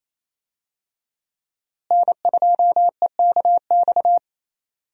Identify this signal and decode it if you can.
Morse code, heard as a keyed tone, about 28 wpm, 700 Hz. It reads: N2EKX
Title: USA Call Signs - 5 Characters - Repeated 2x - 28wpm